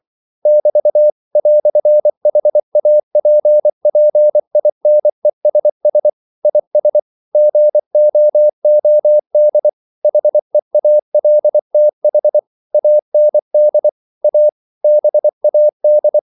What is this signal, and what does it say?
Morse code, heard as a keyed tone, about 24 wpm, 600 Hz. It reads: <BT> "HAPPINESS IS GOOD HEALTH AND A BAD
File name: CW / Practice / EnglishQuotations0324